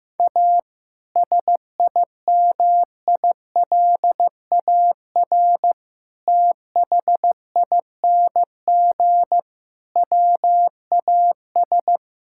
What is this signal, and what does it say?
Morse code, heard as a keyed tone, about 15 wpm, 700 Hz. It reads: A SIMILAR THING WAS
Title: Sentences from Top 1000 Words - Review Part 5 - 15wpm